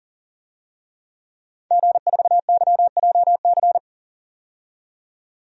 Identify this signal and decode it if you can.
Morse code, heard as a keyed tone, about 40 wpm, 700 Hz. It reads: G4YJC
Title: DXCC-100 Call Signs - 5 Characters - 40wpm